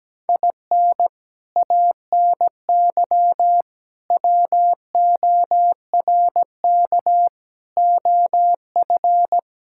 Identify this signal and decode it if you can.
Morse code, heard as a keyed tone, about 17 wpm, 700 Hz. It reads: IN ANY WORK OF
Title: Sentences from Top 1000 Words - Review Part 6 - 17wpm